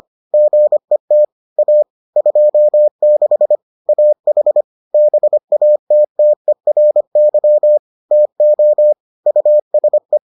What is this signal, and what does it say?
Morse code, heard as a keyed tone, about 25 wpm, 600 Hz. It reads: GET A 26 AH BATTERY TO USE